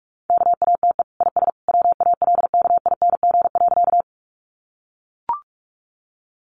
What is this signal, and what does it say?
Morse code, heard as a keyed tone, about 44 wpm, 700 Hz. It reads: KATE IS WALKING.
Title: Easy Phrases 45wpm